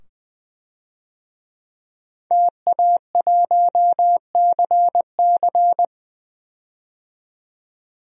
Morse code, heard as a keyed tone, about 20 wpm, 700 Hz. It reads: TA1CC
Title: DXCC-100 Call Signs - 5 Characters - 20wpm